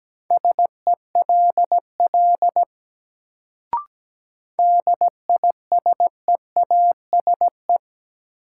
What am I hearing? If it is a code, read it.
Morse code, heard as a keyed tone, about 17 wpm, 700 Hz. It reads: SELL DISEASE
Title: Sentences from Top 1000 Words - Review Part 6 - New Word Drill - 17wpm